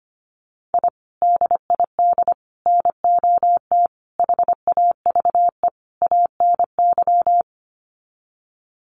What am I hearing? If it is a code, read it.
Morse code, heard as a keyed tone, about 25 wpm, 700 Hz. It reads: I DID NOT HAVE ANY
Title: Sets of 5 Words Encore 25wpm